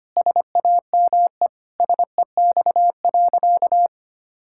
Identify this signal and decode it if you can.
Morse code, heard as a keyed tone, about 25 wpm, 700 Hz. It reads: SAME SEX.
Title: Sentences from Top 1600 Words 25wpm